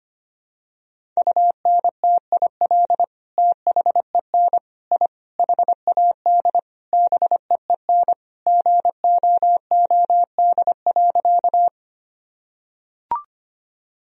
Morse code, heard as a keyed tone, about 25 wpm, 700 Hz. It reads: UNTIL THEN I HAD BEEN GOOD.
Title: Sentences from Top 1000 Words - Review Part 1 - 25wpm